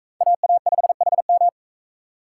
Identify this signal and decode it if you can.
Morse code, heard as a keyed tone, about 41 wpm, 700 Hz. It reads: AA5HM